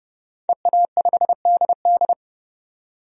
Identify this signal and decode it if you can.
Morse code, heard as a keyed tone, about 30 wpm, 700 Hz. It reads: EA5DD
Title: DXCC-100 Call Signs - 5 Characters - 30wpm